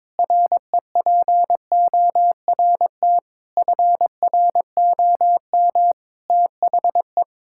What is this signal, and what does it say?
Morse code, heard as a keyed tone, about 22 wpm, 700 Hz. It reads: REPORT FROM THE
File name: Sentences with 7-Word Phrases - 22wpm